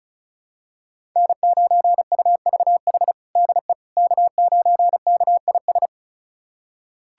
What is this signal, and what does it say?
Morse code, heard as a keyed tone, about 36 wpm, 700 Hz. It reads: N9UVH DE K9KIS
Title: USA Callsign DE Callsign - 5 Characters - 35wpm